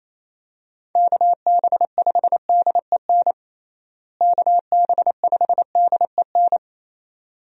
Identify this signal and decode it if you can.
Morse code, heard as a keyed tone, about 28 wpm, 700 Hz. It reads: KB5DEN KB5DEN
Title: USA Call Signs - 6 Characters - Repeated 2x - 28wpm